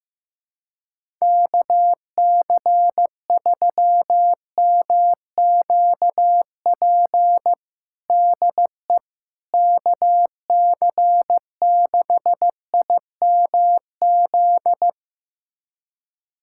Morse code, heard as a keyed tone, about 15 wpm, 700 Hz. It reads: KC3MQP DE KC6IMZ
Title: USA Callsign DE Callsign 15wpm